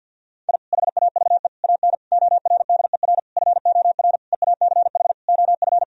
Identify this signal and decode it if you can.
Morse code, heard as a keyed tone, about 50 wpm, 700 Hz. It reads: I HAVE AN ORDER FOR EACH OF
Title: Sentences from Top 1000 Words - Review Part 2 - 50wpm